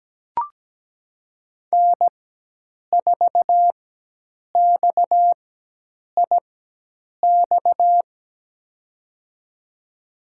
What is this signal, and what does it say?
Morse code, heard as a keyed tone, about 17 wpm, spaced as for 8 wpm, 700 Hz. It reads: N4XIX
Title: USA Call Signs - 5 Characters - 4x Character Spacing - 17wpm